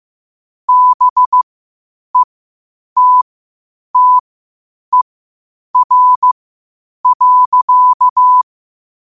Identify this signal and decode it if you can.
Morse code, heard as a keyed tone, about 15 wpm, spaced as for 8 wpm, 1000 Hz. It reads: BETTER.
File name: qotdfiles / 8wpm